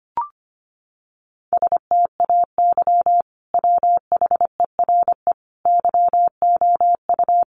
Morse code, heard as a keyed tone, about 25 wpm, 700 Hz. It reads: STAY WHERE YOU
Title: Sentences from Top 500 Words 25wpm